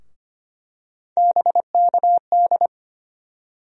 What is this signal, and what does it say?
Morse code, heard as a keyed tone, about 25 wpm, 700 Hz. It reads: BKD
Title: USA Call Sign Postfixes - 3 Letters 25wpm